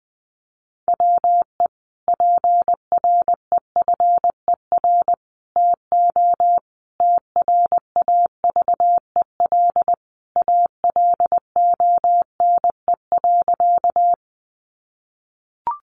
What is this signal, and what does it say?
Morse code, heard as a keyed tone, about 20 wpm, 700 Hz. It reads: WE PREFER TO TRAVEL ALONE.
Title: Sentences from Top 1100 Words 20wpm